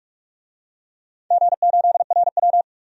Morse code, heard as a keyed tone, about 44 wpm, 700 Hz. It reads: G8RW
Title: DXCC-100 Call Signs - 4 Characters - 45wpm